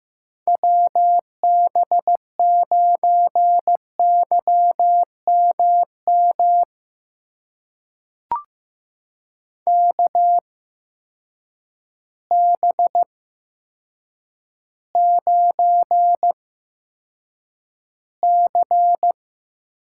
Morse code, heard as a keyed tone, about 15 wpm, spaced as for 8 wpm, 700 Hz. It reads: WB9YMM KB9C
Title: USA Call Signs - 6 Characters - 8x Character Spacing - 15wpm